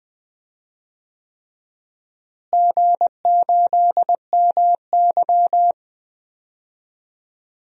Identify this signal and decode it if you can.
Morse code, heard as a keyed tone, about 20 wpm, 700 Hz. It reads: G8MY
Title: DXCC-100 Call Signs - 4 Characters - Repeated 3x - 20wpm